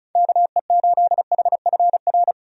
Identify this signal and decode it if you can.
Morse code, heard as a keyed tone, about 36 wpm, 700 Hz. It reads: KE8HFR
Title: USA Call Signs - 6 Characters - 35wpm